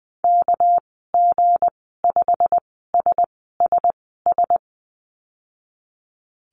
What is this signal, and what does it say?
Morse code, heard as a keyed tone, about 19 wpm, spaced as for 14 wpm, 700 Hz. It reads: KG5SSS
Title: 200 Lesson 32g - Letter k - Call Signs 20wpm